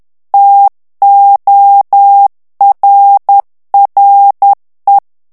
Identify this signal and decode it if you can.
Morse code, heard as a keyed tone, about 10 wpm, 800 Hz. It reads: TORRE
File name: ITALIEN texte clair 1 vitesse 10 mots minute